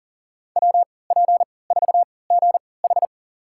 Morse code, heard as a keyed tone, about 37 wpm, spaced as for 22 wpm, 700 Hz. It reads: WP4GH
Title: USA Call Signs - 5 Characters - 3x Character Spacing - 40wpm